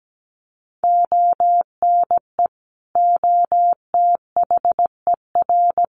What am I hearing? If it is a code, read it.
Morse code, heard as a keyed tone, about 17 wpm, 700 Hz. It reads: ONE OTHER
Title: Sentences from Top 400 Words 17wpm